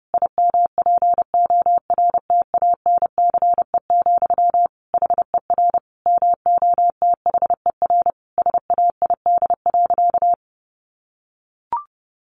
Morse code, heard as a keyed tone, about 30 wpm, 700 Hz. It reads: IMPORTANCE, HER MOTHER SAID.
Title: Sentences from Top 1500 Words 30wpm